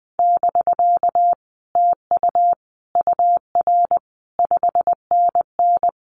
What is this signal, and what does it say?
Morse code, heard as a keyed tone, about 19 wpm, 700 Hz. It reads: <BK> TU UR 5NN
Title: Common QSO Phrases Encore 20wpm